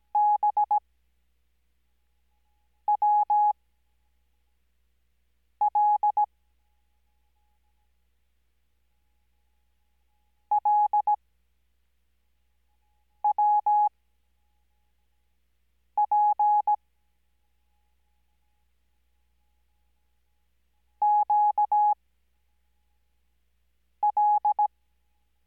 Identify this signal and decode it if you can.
Morse code, heard as a keyed tone, about 17 wpm, spaced as for 4 wpm, 840 Hz. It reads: BWL LWP QL
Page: Farnsworth Spacing: 15WPM Character speed Spaced at 5WPM.